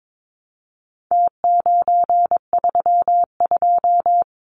Morse code, heard as a keyed tone, about 22 wpm, 700 Hz. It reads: T932
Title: Numbers - Contest Style - 4-Digit 22wpm